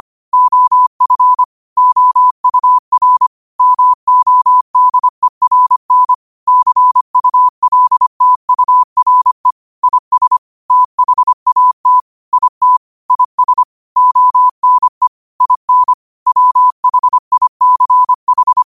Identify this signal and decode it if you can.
Morse code, heard as a keyed tone, about 25 wpm, 1000 Hz. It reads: OF OUR MODERN CULTURE IS THAT IT IS ONE IN WHICH
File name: qotdfiles / 25wpm